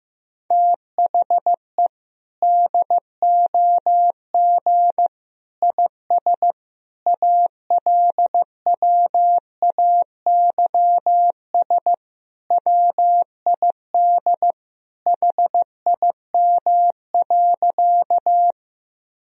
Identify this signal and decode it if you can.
Morse code, heard as a keyed tone, about 15 wpm, 700 Hz. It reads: THE DOG IS ALWAYS WID HIM.
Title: Sentences from Top 400 Words and Abbreviations 15wpm